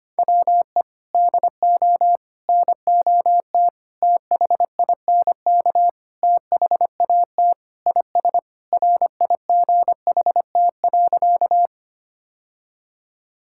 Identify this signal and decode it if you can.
Morse code, heard as a keyed tone, about 25 wpm, 700 Hz. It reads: WE DO NOT THINK THAT IS RIGHT.
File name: practice / 25wpm / Sentences with 4-Word Phrases - 25wpm